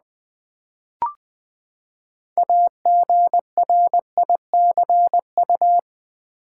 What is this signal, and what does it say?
Morse code, heard as a keyed tone, about 20 wpm, 700 Hz. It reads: AGRICU
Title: Build A Word - 6 Letters At A Time - 12-Letter Words - 20wpm